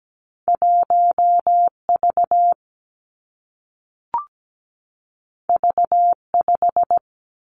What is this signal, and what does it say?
Morse code, heard as a keyed tone, about 17 wpm, 700 Hz. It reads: 1V V5
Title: Two Challenging Letter-Number Combinations 17wpm